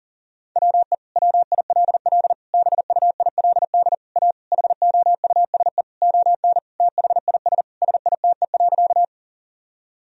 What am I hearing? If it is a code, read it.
Morse code, heard as a keyed tone, about 40 wpm, 700 Hz. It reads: WE WILL BUILD A HOUSE ON THIS SITE.
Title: Sentences from Top 1000 Words - Review Part 4 - 40wpm